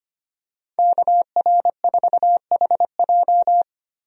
Morse code, heard as a keyed tone, about 25 wpm, 700 Hz. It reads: KR4HJ
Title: USA Call Signs - 5 Characters - 25wpm